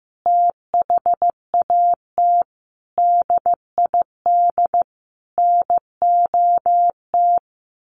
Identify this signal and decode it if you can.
Morse code, heard as a keyed tone, about 15 wpm, 700 Hz. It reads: THAT DID NOT
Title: Sentences from Top 800 Words 15wpm